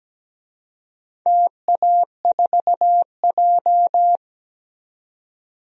Morse code, heard as a keyed tone, about 17 wpm, 700 Hz. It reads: TA4J
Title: DXCC-100 Call Signs - 4 Characters - 17wpm